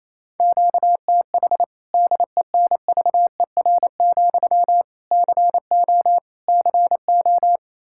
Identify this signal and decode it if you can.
Morse code, heard as a keyed tone, about 28 wpm, 700 Hz. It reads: QTH DENVER, CO CO
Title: Practice QSO - Exchange 1 - Beginning - 28wpm